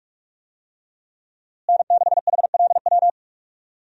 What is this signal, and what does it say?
Morse code, heard as a keyed tone, about 44 wpm, 700 Hz. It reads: N6HLW
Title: USA Call Signs - 5 Characters - Repeated 3x - 45wpm